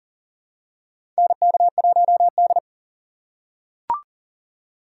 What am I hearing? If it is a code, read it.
Morse code, heard as a keyed tone, about 40 wpm, 700 Hz. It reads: NK1D
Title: USA Call Signs - 4 Characters - 6x Character Spacing - 40wpm